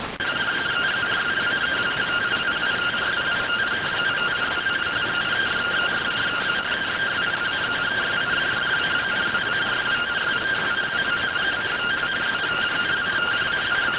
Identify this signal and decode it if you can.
Two-tone teleprinter signal, about 170 Hz shift.